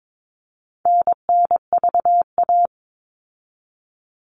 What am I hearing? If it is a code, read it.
Morse code, heard as a keyed tone, about 22 wpm, 700 Hz. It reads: NNVA
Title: All Cut Numbers - 4-Digit 22wpm